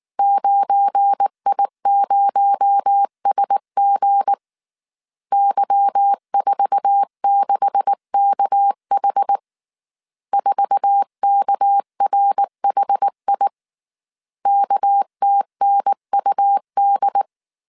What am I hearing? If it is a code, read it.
Morse code, heard as a keyed tone, about 19 wpm, 800 Hz. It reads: 9I0SG Y46KH 4KRHI KTNUD
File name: CW / DET / 1 9 1500C S A 18wpm 2459